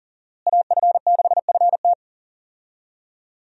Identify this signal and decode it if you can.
Morse code, heard as a keyed tone, about 40 wpm, 690 Hz. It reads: AF6FT
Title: USA Call Signs - 5 Characters - Repeated 3x - 40wpm